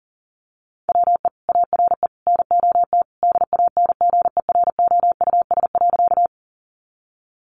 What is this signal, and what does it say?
Morse code, heard as a keyed tone, about 37 wpm, 700 Hz. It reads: WE ARE NOT DANGEROUS.
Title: Sentences from Top 1200 Words 40wpm